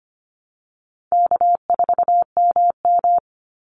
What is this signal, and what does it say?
Morse code, heard as a keyed tone, about 25 wpm, 700 Hz. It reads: K4MM
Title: Easy Call Signs Repeated 3x - 25wpm